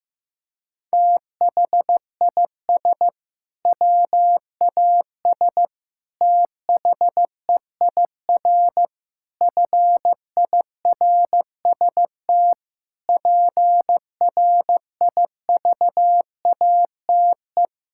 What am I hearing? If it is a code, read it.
Morse code, heard as a keyed tone, about 16 wpm, 700 Hz. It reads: THIS WAS THEIR FIRST PRIVATE
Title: Sentences from Top 1600 Words 15wpm